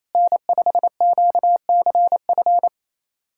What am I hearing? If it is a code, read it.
Morse code, heard as a keyed tone, about 28 wpm, 700 Hz. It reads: N5QCF
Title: USA Callsign DE Callsign 28wpm